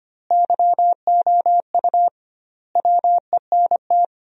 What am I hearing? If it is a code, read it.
Morse code, heard as a keyed tone, about 25 wpm, 700 Hz. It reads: YOU WENT
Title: 2x Word-Spacing - Sentences from Top 400 Words - 25wpm